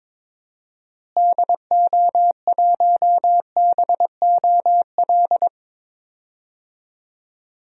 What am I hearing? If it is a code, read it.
Morse code, heard as a keyed tone, about 22 wpm, 700 Hz. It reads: DO1BOL
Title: DXCC-100 Call Signs - 6 Characters - Repeated 2x - 22wpm